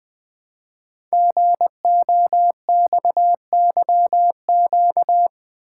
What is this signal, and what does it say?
Morse code, heard as a keyed tone, about 20 wpm, 700 Hz. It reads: GOXYQ
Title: Code Groups - 5 Letter:Numbers - 20wpm